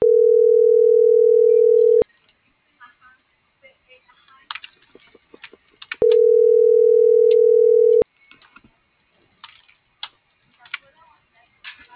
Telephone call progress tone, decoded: ringback tone

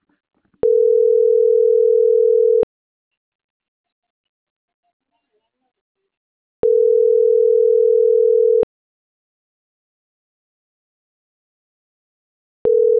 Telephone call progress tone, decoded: ringback tone

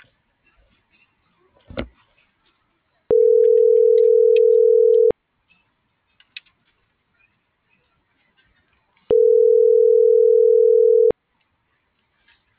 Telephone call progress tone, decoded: ringback tone

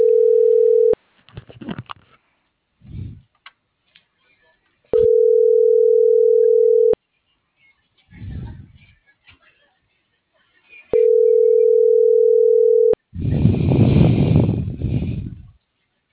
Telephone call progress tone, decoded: ringback tone

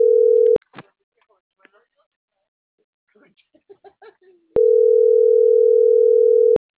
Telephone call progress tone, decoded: ringback tone

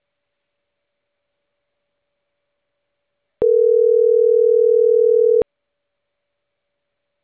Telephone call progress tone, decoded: ringback tone

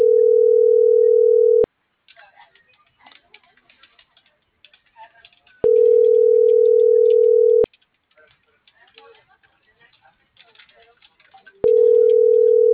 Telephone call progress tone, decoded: ringback tone

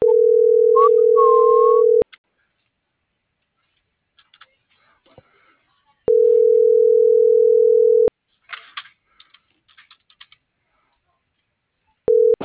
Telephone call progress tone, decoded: ringback tone

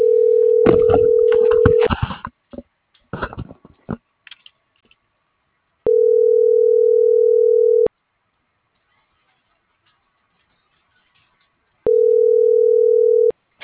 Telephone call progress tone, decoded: ringback tone

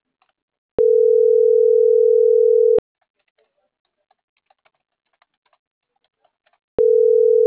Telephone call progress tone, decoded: ringback tone

